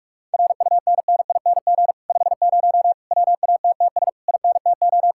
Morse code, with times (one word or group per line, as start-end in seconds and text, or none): 0.3-1.9 RUNNING
2.1-2.9 50
3.1-4.1 WATTS
4.3-5.1 INTO